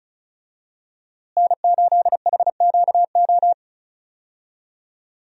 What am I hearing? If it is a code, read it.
Morse code, heard as a keyed tone, about 36 wpm, 700 Hz. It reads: N8HQO
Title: USA Call Signs - 5 Characters - Repeated 3x - 35wpm